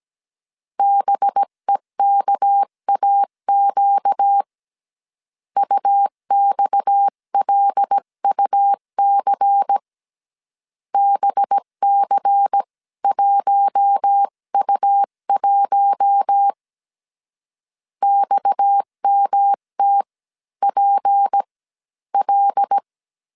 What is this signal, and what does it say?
Morse code, heard as a keyed tone, about 17 wpm, spaced as for 14 wpm, 800 Hz. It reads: BEKAQ UXLUC BC1U1 XMTPL
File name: CW / DET / 1 6 1500C S C 16wpm 2733